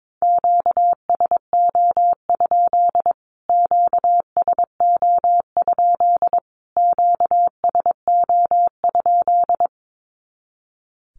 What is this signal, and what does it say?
Morse code, heard as a keyed tone, about 22 wpm, 700 Hz. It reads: QSO? QSO? QSO?
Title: Q-Codes Repeated 3x 22wpm